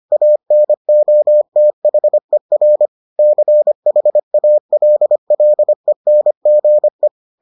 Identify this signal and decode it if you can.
Morse code, heard as a keyed tone, about 25 wpm, 600 Hz. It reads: ANOTHER CHALLENGE